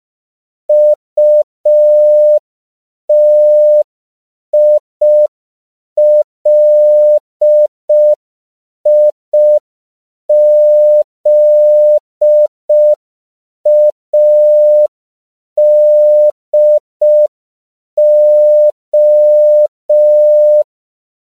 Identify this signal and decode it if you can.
Morse code, heard as a keyed tone, about 5 wpm, 600 Hz. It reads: UTILIZADO
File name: CW / 5PPM